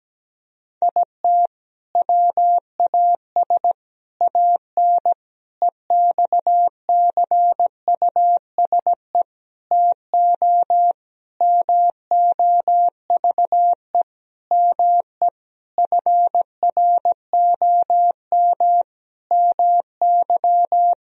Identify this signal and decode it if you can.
Morse code, heard as a keyed tone, about 17 wpm, 700 Hz. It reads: IT WAS AN EXCUSE TO MOVE ME FROM MY
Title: Sentences from Top 1000 Words - Review Part 1 - 17wpm